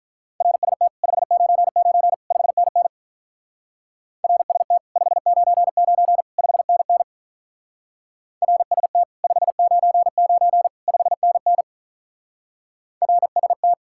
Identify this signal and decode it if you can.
Morse code, heard as a keyed tone, about 43 wpm, 700 Hz. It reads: RST 599 5NN RST 599 5NN RST 599 5NN RST
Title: Speed-Racing Common QSO Phrases - 35wpm